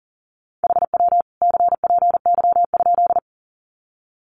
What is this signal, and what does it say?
Morse code, heard as a keyed tone, about 40 wpm, 700 Hz. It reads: HW CPY?
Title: Common QSO Phrases 40wpm